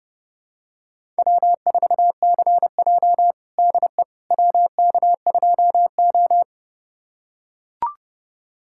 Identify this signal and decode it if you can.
Morse code, heard as a keyed tone, about 32 wpm, 700 Hz. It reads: W4CJ DE WK2O
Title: USA Callsign DE Callsign - 4 Characters - 30wpm